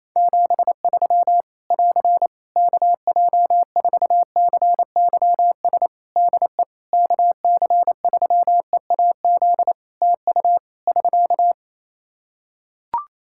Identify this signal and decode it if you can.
Morse code, heard as a keyed tone, about 28 wpm, 700 Hz. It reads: 73 <AR> KJ4CYS DE KC3EAZ TU <SK>
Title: Practice QSO - Exchange 4 - Ending - 28wpm